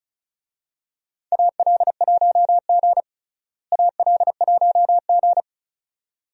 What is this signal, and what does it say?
Morse code, heard as a keyed tone, about 36 wpm, 700 Hz. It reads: AL1G AL1G
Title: USA Call Signs - 4 Characters - Repeated 2x - 35wpm